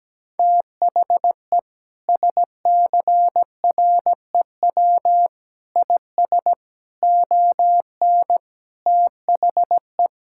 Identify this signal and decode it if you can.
Morse code, heard as a keyed tone, about 17 wpm, 700 Hz. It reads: THE SCREW IS ON THE
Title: Sentences from Top 1600 Words 17wpm